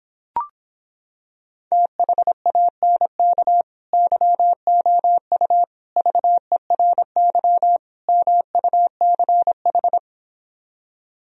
Morse code, heard as a keyed tone, about 26 wpm, 700 Hz. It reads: THANK YOU VERY MUCH